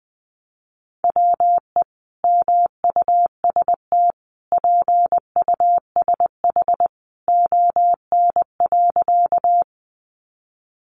Morse code, heard as a keyed tone, about 20 wpm, 700 Hz. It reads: WE MUST PUSH ON.
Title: Sentences from Top 1300 Words 20wpm